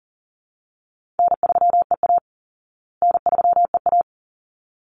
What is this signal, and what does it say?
Morse code, heard as a keyed tone, about 40 wpm, 700 Hz. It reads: N3EA N3EA
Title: Easy Call Signs Repeated 2x - 40wpm